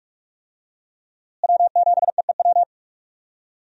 Morse code, heard as a keyed tone, about 42 wpm, 700 Hz. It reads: W7EEW
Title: USA Call Signs - 5 Characters - Repeated 2x - 45wpm